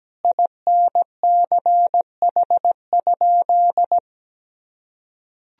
Morse code, heard as a keyed tone, about 17 wpm, 700 Hz. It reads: INCH?
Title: Sentences from Top 1000 Words - Review Part 3 - 17wpm